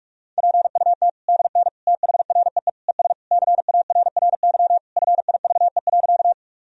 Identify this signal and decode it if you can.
Morse code, heard as a keyed tone, about 46 wpm, 700 Hz. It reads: PUT DN THREE ES CARRY FIVE.
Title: Sentences from Top 400 Words and Abbreviations 45wpm